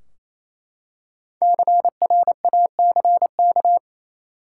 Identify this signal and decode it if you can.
Morse code, heard as a keyed tone, about 28 wpm, 700 Hz. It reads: CRACK
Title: Five Letter Words - 28wpm